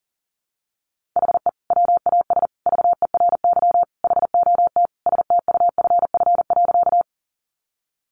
Morse code, heard as a keyed tone, about 40 wpm, 700 Hz. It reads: HE WAS VERY HOT STUFF.
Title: Sentences from Top 1400 Words 40wpm